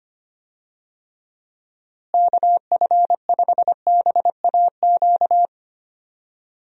Morse code, heard as a keyed tone, about 25 wpm, 700 Hz. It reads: KF5BAQ
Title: USA Call Signs - 6 Characters - Repeated 2x - 25wpm